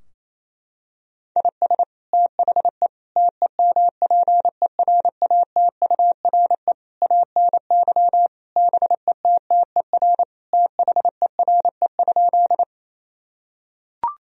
Morse code, heard as a keyed tone, about 28 wpm, 700 Hz. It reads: IS THE TEMPERATURE ANY BETTER THERE?
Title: Sentences from Top 1000 Words - Review Part 4 - 28wpm